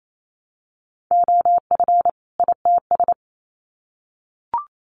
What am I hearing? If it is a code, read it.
Morse code, heard as a keyed tone, about 28 wpm, 700 Hz. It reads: OF ITS
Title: Sets of 2 Words 28wpm